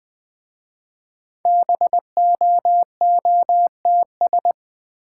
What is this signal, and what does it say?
Morse code, heard as a keyed tone, about 20 wpm, 700 Hz. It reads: BOOTS
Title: Easy Words with Hints - 20wpm